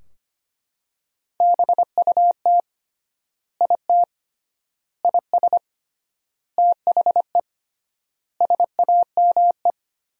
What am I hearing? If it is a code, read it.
Morse code, heard as a keyed tone, about 25 wpm, 700 Hz. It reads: BUT IT IS THE SAME
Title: 3x Word-Spacing - Sentences from Top 300 Words - 25wpm